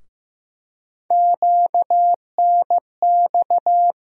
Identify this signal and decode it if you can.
Morse code, heard as a keyed tone, about 15 wpm, 700 Hz. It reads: QNX
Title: USA Call Sign Postfixes - 3 Letters 15wpm